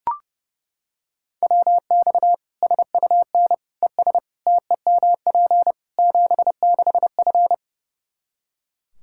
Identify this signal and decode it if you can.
Morse code, heard as a keyed tone, about 30 wpm, 700 Hz. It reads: WX SUN ES TEMP 76F
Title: K1USN SST QSO Practice 30wpm